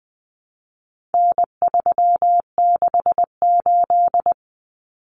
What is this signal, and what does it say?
Morse code, heard as a keyed tone, about 20 wpm, 700 Hz. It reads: N368
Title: Common Cut Numbers - 4-Digit 20wpm